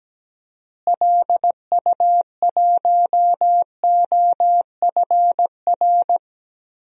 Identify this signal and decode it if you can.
Morse code, heard as a keyed tone, about 17 wpm, 700 Hz. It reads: LU1OFR
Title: DXCC-100 Call Signs - 6 Characters - Repeated 3x - 17wpm